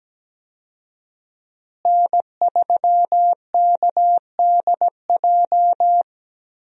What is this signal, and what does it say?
Morse code, heard as a keyed tone, about 17 wpm, 700 Hz. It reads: N3KDJ
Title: USA Call Signs - 5 Characters - Repeated 2x - 17wpm